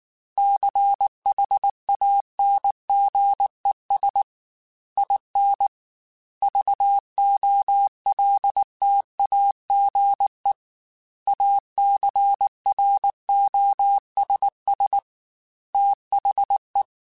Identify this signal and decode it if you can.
Morse code, heard as a keyed tone, about 19 wpm, 800 Hz. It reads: CHANGES IN VOLTAGE ACROSS THE
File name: Technical Subjects / Technical Subjects-mp3s / Reactance @18